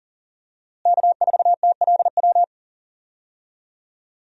Morse code, heard as a keyed tone, about 38 wpm, 700 Hz. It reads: K4TLW
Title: USA Call Signs - 5 Characters - Repeated 2x - 40wpm